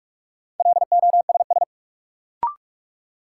Morse code, heard as a keyed tone, about 41 wpm, 700 Hz. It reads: LOSS
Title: Sentences from Top 1400 Words - New Word Drill Alt - 45wpm